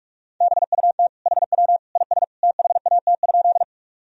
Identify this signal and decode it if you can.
Morse code, heard as a keyed tone, about 46 wpm, 700 Hz. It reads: BUT HW IS THAT?
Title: Sentences from Top 100 Words and Abbreviations 45wpm